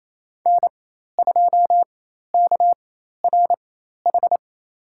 Morse code, heard as a keyed tone, about 28 wpm, spaced as for 13 wpm, 700 Hz. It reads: N2KRH
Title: USA Call Signs - 4x Character Spacing - 28wpm